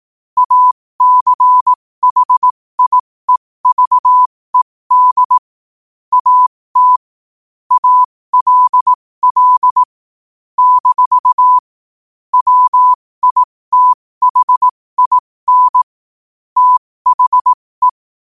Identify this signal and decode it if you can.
Morse code, heard as a keyed tone, about 18 wpm, spaced as for 15 wpm, 1000 Hz. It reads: ACHIEVED AT ALL - WITHIN THE
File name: qotdfiles / 15wpm